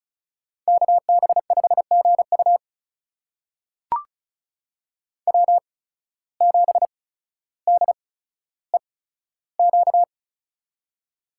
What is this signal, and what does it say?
Morse code, heard as a keyed tone, about 36 wpm, spaced as for 15 wpm, 700 Hz. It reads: KB5GU W7DEQ
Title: USA Call Signs - 5 Characters - 8x Character Spacing - 35wpm